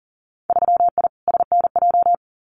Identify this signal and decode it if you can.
Morse code, heard as a keyed tone, about 40 wpm, 700 Hz. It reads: 3I SNJ
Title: Winter Field Day Exchanges Encore 40wpm